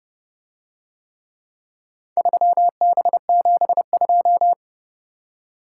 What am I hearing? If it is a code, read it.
Morse code, heard as a keyed tone, about 31 wpm, 700 Hz. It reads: 3B72